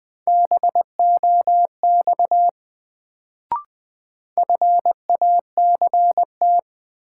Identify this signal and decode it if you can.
Morse code, heard as a keyed tone, about 20 wpm, 700 Hz. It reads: BOX FACT
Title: Sentences from Top 500 Words - New Word Drill - 20wpm